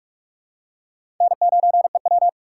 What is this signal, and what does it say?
Morse code, heard as a keyed tone, about 46 wpm, 700 Hz. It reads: N9EW
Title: USA Call Signs - 4 Characters - 3x Character Spacing - 45wpm